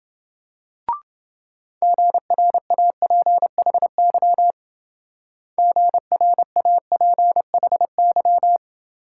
Morse code, heard as a keyed tone, about 25 wpm, 700 Hz. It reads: GRAPHY GRAPHY